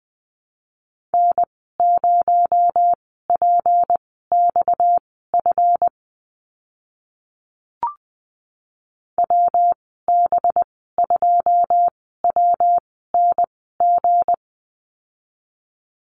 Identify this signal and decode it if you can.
Morse code, heard as a keyed tone, about 20 wpm, spaced as for 15 wpm, 700 Hz. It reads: N0PXF WB2WNG
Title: 231 Lesson 37g - Letter x - Call Signs 20wpm